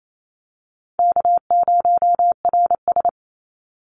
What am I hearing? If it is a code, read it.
Morse code, heard as a keyed tone, about 28 wpm, 700 Hz. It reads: K0RS
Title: NA SOTA Chaser Callsigns 28wpm